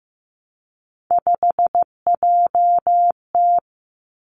Morse code, heard as a keyed tone, about 15 wpm, 700 Hz. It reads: 5JT
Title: Three Challenging Letter-Number Combinations 15wpm